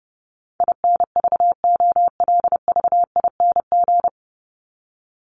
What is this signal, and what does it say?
Morse code, heard as a keyed tone, about 30 wpm, 700 Hz. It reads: INVOLVING
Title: ING Suffix - Top 100 Words with Base Words 30wpm